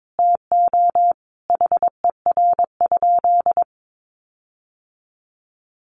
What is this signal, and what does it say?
Morse code, heard as a keyed tone, about 22 wpm, 700 Hz. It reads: TO HER?
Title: Sentences from Top 1400 Words 22wpm